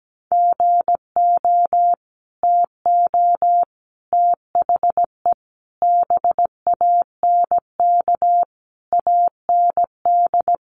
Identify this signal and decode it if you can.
Morse code, heard as a keyed tone, about 17 wpm, 700 Hz. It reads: GO TO THE BANK AND
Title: Sentences from Top 1500 Words 17wpm